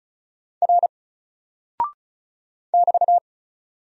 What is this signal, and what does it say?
Morse code, heard as a keyed tone, about 36 wpm, 700 Hz. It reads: R <BT>
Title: Single Character (Letter:Number:Punctuation:Prosign) Alt 35wpm